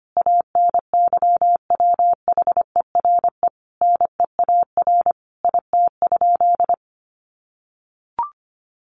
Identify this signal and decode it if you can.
Morse code, heard as a keyed tone, about 25 wpm, 700 Hz. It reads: ANYWHERE NEAR IT?
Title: Sentences from Top 1400 Words 25wpm